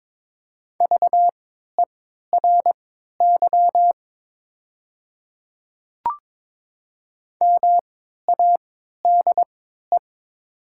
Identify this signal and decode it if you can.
Morse code, heard as a keyed tone, about 22 wpm, spaced as for 12 wpm, 700 Hz. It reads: VERY MADE
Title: 3x Character Spacing - Top 300 Words - 22wpm